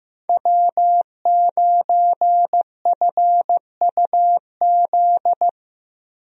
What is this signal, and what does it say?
Morse code, heard as a keyed tone, about 15 wpm, 700 Hz. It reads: W9FUZ
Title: USA Call Signs - 5 Characters - 15wpm